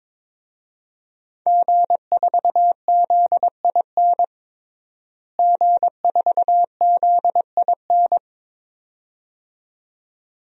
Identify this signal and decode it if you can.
Morse code, heard as a keyed tone, about 22 wpm, 700 Hz. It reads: G4ZIN G4ZIN
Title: DXCC-100 Call Signs - 5 Characters - Repeated 2x - 22wpm